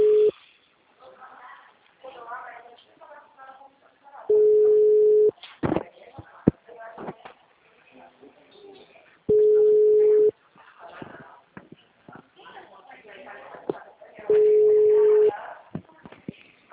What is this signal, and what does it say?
Ringback tone (Europe, 425 Hz), with speech.
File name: out- -2001- - - .51167